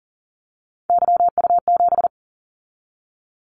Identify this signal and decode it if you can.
Morse code, heard as a keyed tone, about 40 wpm, 700 Hz. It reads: YU7
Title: Callsign Prefixes for Top 25 DXCC - 40wpm